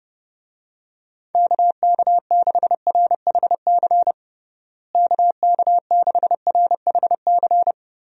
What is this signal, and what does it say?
Morse code, heard as a keyed tone, about 32 wpm, 700 Hz. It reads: KK6RHC KK6RHC
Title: USA Call Signs - 6 Characters - Repeated 3x - 30wpm